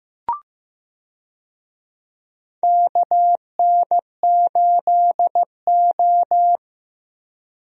Morse code, heard as a keyed tone, about 15 wpm, 700 Hz. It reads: KN8O
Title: USA Call Signs - 4 Characters - Repeated 2x - 15wpm